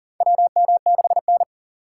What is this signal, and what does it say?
Morse code, heard as a keyed tone, about 40 wpm, 690 Hz. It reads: WM6N
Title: USA Call Signs - 4 Characters - 6x Character Spacing - 40wpm